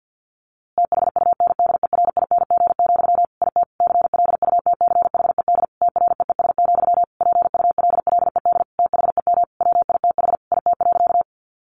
Morse code, heard as a keyed tone, about 50 wpm, 700 Hz. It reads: THUNDERING, IT CLUTCHED TREES, PULLED THEM WITH IT.
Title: A Sound Of Thunder 50wpm